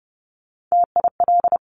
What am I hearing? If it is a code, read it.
Morse code, heard as a keyed tone, about 30 wpm, 700 Hz. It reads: TIL
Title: Top Three Letter Combinations in Words 30wpm